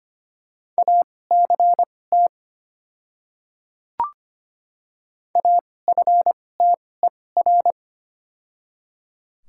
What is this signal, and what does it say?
Morse code, heard as a keyed tone, about 25 wpm, spaced as for 18 wpm, 700 Hz. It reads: ACT AFTER
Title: 2x Character Spacing - Top 200 Words - 25wpm